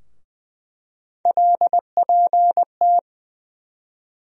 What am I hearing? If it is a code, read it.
Morse code, heard as a keyed tone, about 20 wpm, 700 Hz. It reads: LPT
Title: USA Call Sign Postfixes - 3 Letters 20wpm